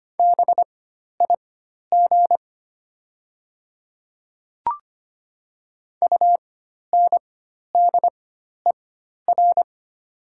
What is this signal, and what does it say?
Morse code, heard as a keyed tone, about 25 wpm, spaced as for 12 wpm, 700 Hz. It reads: BIG UNDER
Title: 4x Character Spacing - Top 500 Words - 25wpm